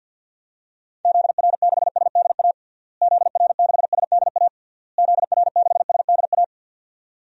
Morse code, heard as a keyed tone, about 51 wpm, 700 Hz. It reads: ZR6SDA ZR6SDA ZR6SDA
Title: DXCC-100 Call Signs - 6 Characters - Repeated 3x - 50wpm